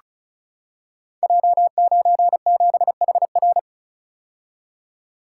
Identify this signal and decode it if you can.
Morse code, heard as a keyed tone, about 36 wpm, 700 Hz. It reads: J97HR
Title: Code Groups - 5 Letter:Numbers - 35wpm